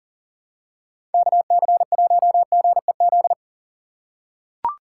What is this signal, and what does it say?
Morse code, heard as a keyed tone, about 40 wpm, 700 Hz. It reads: KC1GEZ
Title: USA Call Signs - 2x Character Spacing - 40wpm